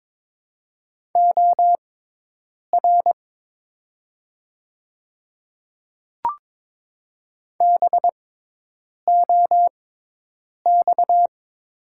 Morse code, heard as a keyed tone, about 22 wpm, spaced as for 8 wpm, 700 Hz. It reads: OR BOX
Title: 6x Character Spacing - Top 500 Words - 22wpm